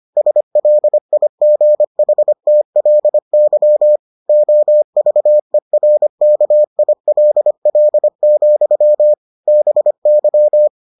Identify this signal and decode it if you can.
Morse code, heard as a keyed tone, about 25 wpm, 600 Hz. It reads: SLIGHTLY OVERKILL, BY